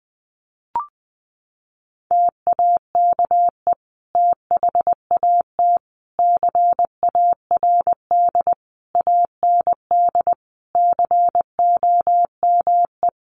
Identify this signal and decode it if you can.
Morse code, heard as a keyed tone, about 20 wpm, 700 Hz. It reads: TAKE THAT CARD AND COME
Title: Sentences from Top 1400 Words 20wpm